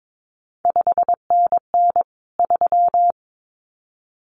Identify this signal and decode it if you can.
Morse code, heard as a keyed tone, about 22 wpm, 700 Hz. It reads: 5NN 3
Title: CQ WW DX Contest Exchange - 22wpm